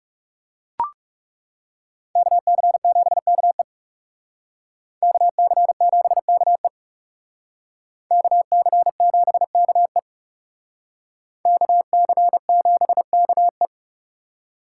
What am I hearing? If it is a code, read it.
Morse code, heard as a keyed tone, about 38 wpm, 700 Hz. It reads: KC7KE KC7KE KC7KE KC7KE